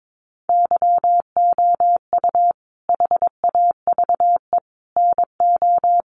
Morse code, heard as a keyed tone, about 22 wpm, 700 Hz. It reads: YOU HAVE NO
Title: Sentences from Top 1100 Words 22wpm